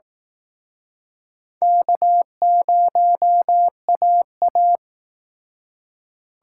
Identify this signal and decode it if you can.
Morse code, heard as a keyed tone, about 19 wpm, 700 Hz. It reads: K0AA